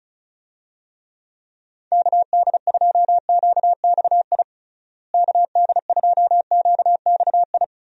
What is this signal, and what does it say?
Morse code, heard as a keyed tone, about 34 wpm, 700 Hz. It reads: KD2QXI KD2QXI
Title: USA Call Signs - 6 Characters - Repeated 3x - 35wpm